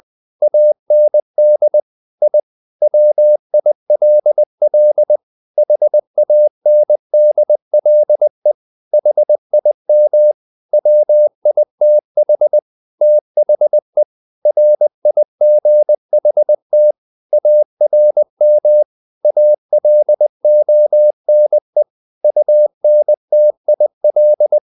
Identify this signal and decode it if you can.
Morse code, heard as a keyed tone, about 20 wpm, 600 Hz. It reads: AND I WILL HANDLE HIM WITH THE RIGHT ARM ALONE UNTIL
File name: the old man and the sea / the old man and the sea part 1 20wpm